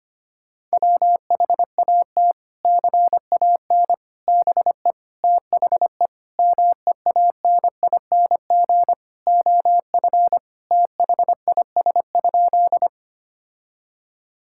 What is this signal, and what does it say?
Morse code, heard as a keyed tone, about 25 wpm, 700 Hz. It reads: WHAT CAN BE THE MEANING OF THIS?
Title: Sentences from Top 1000 Words - Review Part 6 - 25wpm